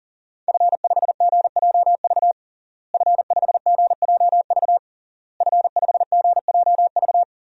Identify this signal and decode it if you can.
Morse code, heard as a keyed tone, about 40 wpm, 700 Hz. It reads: F5GJV F5GJV F5GJV
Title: DXCC-100 Call Signs - 5 Characters - Repeated 3x - 40wpm